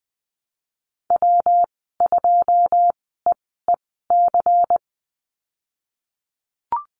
Morse code, heard as a keyed tone, about 20 wpm, spaced as for 15 wpm, 700 Hz. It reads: W2EEC
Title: 186 Lesson 30g - Letter b - Call Signs 20wpm